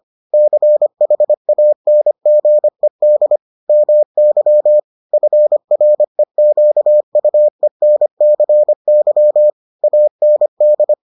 Morse code, heard as a keyed tone, about 25 wpm, 600 Hz. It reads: CHANGED MY FREQUENCY AND